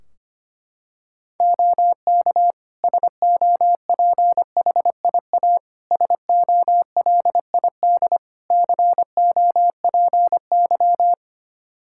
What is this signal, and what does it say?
Morse code, heard as a keyed tone, about 25 wpm, 700 Hz. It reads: OK SOPHIA SOLID COPY
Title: Practice QSO - Exchange 3 - More Info - 25wpm